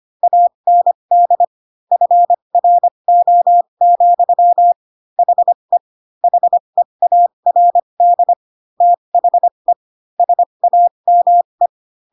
Morse code, heard as a keyed tone, about 25 wpm, spaced as for 22 wpm, 700 Hz. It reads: AND FRO, HE HEARD THE SAME